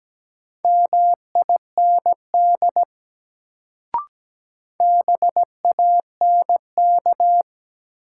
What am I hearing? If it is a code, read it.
Morse code, heard as a keyed tone, about 17 wpm, 700 Hz. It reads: MIND BANK
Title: Sets of 5 Words - New Word Drill - 17wpm